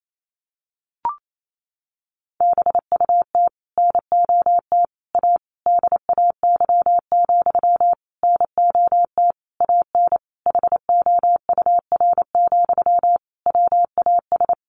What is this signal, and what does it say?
Morse code, heard as a keyed tone, about 28 wpm, 700 Hz. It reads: BUT NOT A DAY, NOT AN HOUR, WAS
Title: Sentences from Top 1400 Words 28wpm